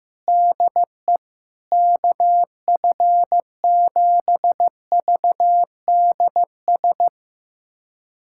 Morse code, heard as a keyed tone, about 15 wpm, 700 Hz. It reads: DE KF7VDS
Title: USA Callsign DE Callsign - 6 Characters - 15wpm